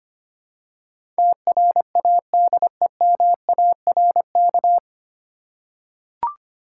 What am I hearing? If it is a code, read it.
Morse code, heard as a keyed tone, about 25 wpm, 700 Hz. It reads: TRADEMARK
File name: Nine Letter Words - 25wpm